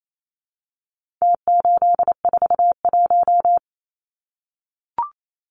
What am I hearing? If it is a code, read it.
Morse code, heard as a keyed tone, about 28 wpm, 700 Hz. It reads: T841
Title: Numbers - Contest Style - 4-Digit 28wpm